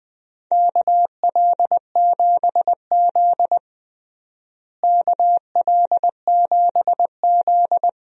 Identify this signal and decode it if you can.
Morse code, heard as a keyed tone, about 21 wpm, 700 Hz. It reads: KL7Z KL7Z
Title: DXCC-100 Call Signs - 4 Characters - Repeated 2x - 20wpm